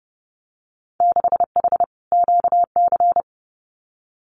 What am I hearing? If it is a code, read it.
Morse code, heard as a keyed tone, about 30 wpm, 700 Hz. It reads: 6H QC
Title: Winter Field Day Exchanges Encore 30wpm